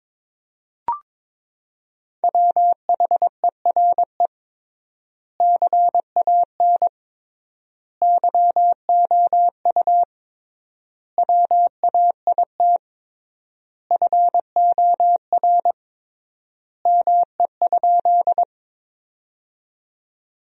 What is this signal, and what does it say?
Morse code, heard as a keyed tone, about 21 wpm, 700 Hz. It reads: WHERE CAN YOU WAIT FOR ME?
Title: 3x Word-Spacing - Sentences from Top 500 Words - 22wpm